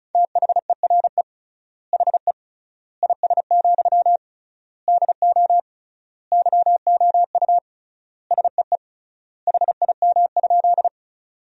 Morse code, heard as a keyed tone, about 36 wpm, 700 Hz. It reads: THERE HE IS, DO YOU SEE HIM?
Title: 3x Word-Spacing - Sentences from Top 100 Words - 35wpm